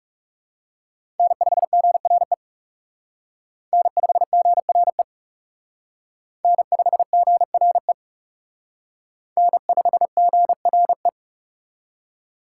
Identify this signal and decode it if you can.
Morse code, heard as a keyed tone, about 36 wpm, 700 Hz. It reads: N5GRE N5GRE N5GRE N5GRE
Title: Speed Racing - USA Call Signs 30wpm